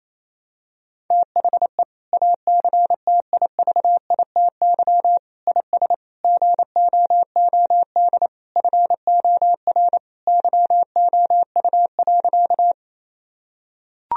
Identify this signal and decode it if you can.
Morse code, heard as a keyed tone, about 28 wpm, 700 Hz. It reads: THE ACTIVITY IS GOOD FOR YOU.
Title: Sentences from Top 1000 Words - Review Part 4 - 28wpm